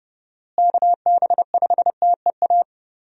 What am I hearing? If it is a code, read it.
Morse code, heard as a keyed tone, about 29 wpm, 700 Hz. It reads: KB5TEA
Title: USA Call Signs - 6 Characters - Repeated 3x - 30wpm